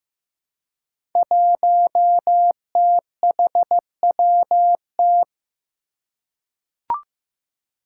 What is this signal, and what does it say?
Morse code, heard as a keyed tone, about 15 wpm, 700 Hz. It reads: 1THWT
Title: Five Challenging Letter-Number Combinations 15wpm